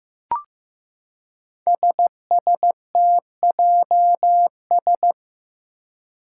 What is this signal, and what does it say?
Morse code, heard as a keyed tone, about 15 wpm, 700 Hz. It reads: SSTJS
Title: Five Challenging Letter-Number Combinations 15wpm